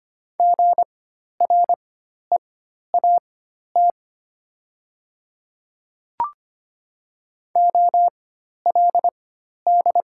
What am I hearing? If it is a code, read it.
Morse code, heard as a keyed tone, about 25 wpm, spaced as for 12 wpm, 700 Hz. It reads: GREAT OLD
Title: 4x Character Spacing - Top 200 Words - 25wpm